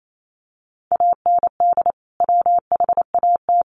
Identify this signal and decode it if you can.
Morse code, heard as a keyed tone, about 28 wpm, 700 Hz. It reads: AND WHAT
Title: Sentences from Top 1500 Words 28wpm